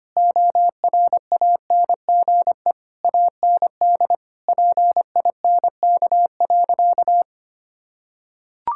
Morse code, heard as a keyed tone, about 25 wpm, 700 Hz. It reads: ORANGE AND PINK.
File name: practice / 25wpm / Difficult Sentences with Hints - 25wpm